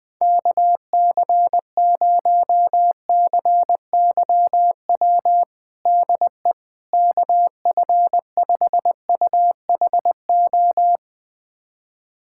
Morse code, heard as a keyed tone, about 21 wpm, 700 Hz. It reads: KC0CYW DE KF5UHO
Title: USA Callsign DE Callsign - 6 Characters - 20wpm